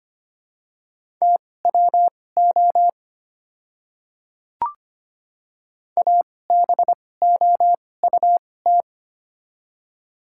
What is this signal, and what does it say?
Morse code, heard as a keyed tone, about 25 wpm, spaced as for 18 wpm, 700 Hz. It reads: TWO ABOUT
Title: 2x Character Spacing - Top 100 Words - 25wpm